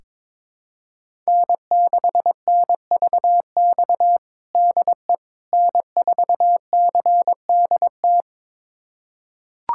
Morse code, heard as a keyed tone, about 22 wpm, 700 Hz. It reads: N6NVX DE N4CDT
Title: USA Callsign DE Callsign - 5 Characters - 22wpm